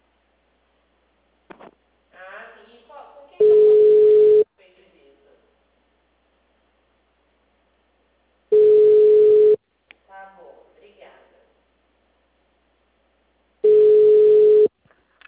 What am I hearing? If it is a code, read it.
Ringback tone (Europe, 425 Hz), with speech.